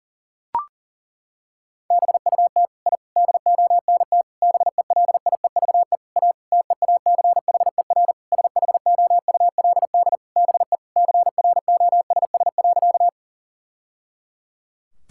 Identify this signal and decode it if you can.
Morse code, heard as a keyed tone, about 40 wpm, 700 Hz. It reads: BUT I DONT BELIEVE A TEACHER SHOULD BE CROSS.
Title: Sentences from Top 1000 Words - Review Part 6 - 40wpm